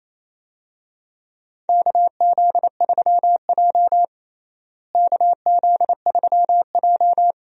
Morse code, heard as a keyed tone, about 27 wpm, 700 Hz. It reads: KZ3J KZ3J
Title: USA Call Signs - 4 Characters - Repeated 3x - 28wpm